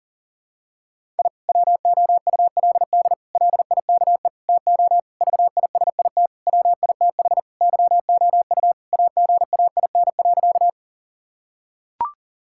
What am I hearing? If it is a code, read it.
Morse code, heard as a keyed tone, about 40 wpm, 700 Hz. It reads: I WOULD LIKE TO VISIT WITH YOU AGAIN.
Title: Sentences from Top 1000 Words - Review Part 6 - 40wpm